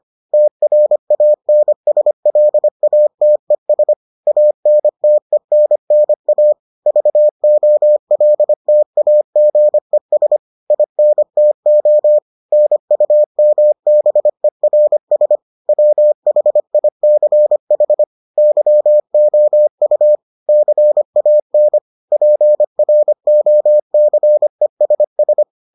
Morse code, heard as a keyed tone, about 25 wpm, 600 Hz. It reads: TRANSLATES ANTENNA VOLTAGES INTO NUMBERS WHICH YOU CAN PROCESS